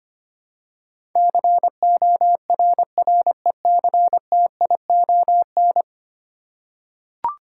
Morse code, heard as a keyed tone, about 25 wpm, 700 Hz. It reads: CORRECTION
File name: Ten Letter Words - 25wpm